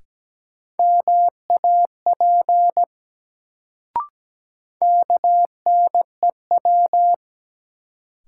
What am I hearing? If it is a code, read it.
Morse code, heard as a keyed tone, about 17 wpm, 700 Hz. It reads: MAP KNEW
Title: Sentences from Top 400 Words - New Word Drill - 17wpm